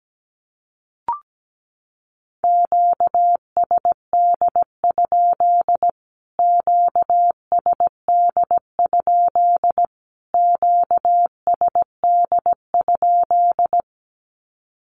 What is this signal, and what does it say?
Morse code, heard as a keyed tone, about 17 wpm, 700 Hz. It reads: QSD? QSD? QSD?
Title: Q-Codes Repeated 3x 17wpm